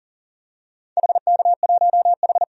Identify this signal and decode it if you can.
Morse code, heard as a keyed tone, about 39 wpm, 700 Hz. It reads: HK1H
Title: DXCC-100 Call Signs - 4 Characters - 40wpm